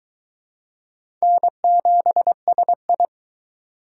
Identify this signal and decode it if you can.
Morse code, heard as a keyed tone, about 23 wpm, 700 Hz. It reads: N7SI